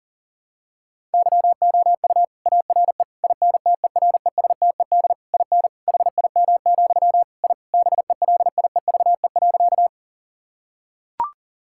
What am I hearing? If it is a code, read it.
Morse code, heard as a keyed tone, about 40 wpm, 700 Hz. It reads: YOU ARE INTERESTED IN HIM, I BELIEVE.
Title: Sentences from Top 1000 Words - Review Part 6 - 40wpm